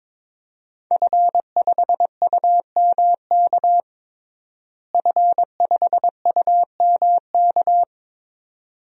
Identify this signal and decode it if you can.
Morse code, heard as a keyed tone, about 21 wpm, 700 Hz. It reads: F5UMK F5UMK
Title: DXCC-100 Call Signs - 5 Characters - Repeated 2x - 22wpm